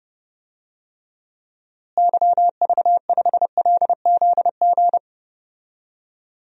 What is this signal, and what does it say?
Morse code, heard as a keyed tone, about 30 wpm, 700 Hz. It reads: YV5LZG
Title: DXCC-100 Call Signs - 6 Characters - Repeated 3x - 30wpm